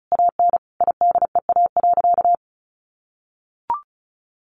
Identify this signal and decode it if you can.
Morse code, heard as a keyed tone, about 34 wpm, 700 Hz. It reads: AN IDEA.
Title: Additional Easy Phrases 35wpm